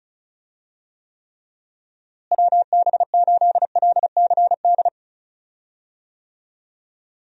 Morse code, heard as a keyed tone, about 36 wpm, 700 Hz. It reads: WB8LCD
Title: USA Call Signs - 6 Characters - Repeated 2x - 35wpm